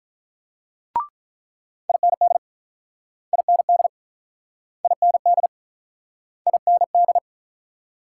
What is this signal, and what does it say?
Morse code, heard as a keyed tone, about 48 wpm, 700 Hz. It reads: IND IND IND IND